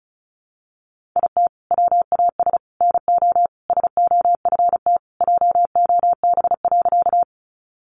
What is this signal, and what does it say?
Morse code, heard as a keyed tone, about 34 wpm, 700 Hz. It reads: IT WAS NO SOFT JOB.
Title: Sentences from Top 1500 Words 35wpm